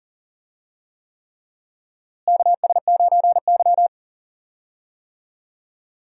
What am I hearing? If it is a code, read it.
Morse code, heard as a keyed tone, about 40 wpm, 700 Hz. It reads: KS9Y
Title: USA Call Signs - 4 Characters - Repeated 3x - 40wpm